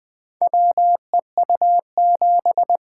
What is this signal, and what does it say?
Morse code, heard as a keyed tone, about 21 wpm, 700 Hz. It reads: WEU7
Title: Four Letter-Number Combinations - 20wpm